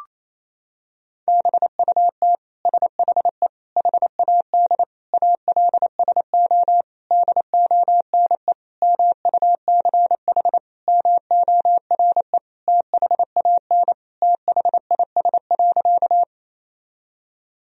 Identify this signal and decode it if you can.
Morse code, heard as a keyed tone, about 28 wpm, 700 Hz. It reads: BUT SHE HAD ALSO DONE MUCH MORE THAN THIS.
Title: Sentences from Top 1000 Words - Review Part 1 - 28wpm